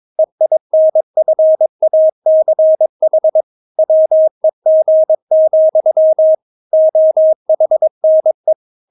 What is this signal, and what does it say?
Morse code, heard as a keyed tone, about 22 wpm, 630 Hz. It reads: EINFACH WEG, OHNE
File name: Witz3 22wpm